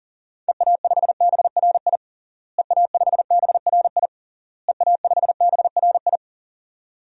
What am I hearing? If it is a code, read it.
Morse code, heard as a keyed tone, about 40 wpm, 700 Hz. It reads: EA5BRI EA5BRI EA5BRI
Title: DXCC-100 Call Signs - 6 Characters - Repeated 3x - 40wpm